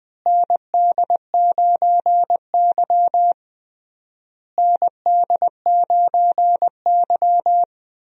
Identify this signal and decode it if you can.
Morse code, heard as a keyed tone, about 21 wpm, 700 Hz. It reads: ND9Y ND9Y
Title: USA Call Signs - 4 Characters - Repeated 3x - 20wpm